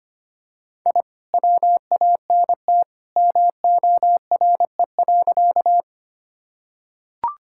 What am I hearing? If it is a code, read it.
Morse code, heard as a keyed tone, about 25 wpm, 700 Hz. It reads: I WANT MORE.
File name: practice / 25wpm / Sentences with 2-Word Phrases - 25wpm